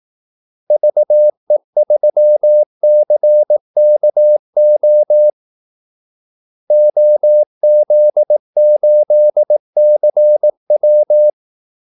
Morse code, heard as a keyed tone, about 18 wpm, 600 Hz. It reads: VE3CKO OZ8CW
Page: Kaldesignaler 18 wpm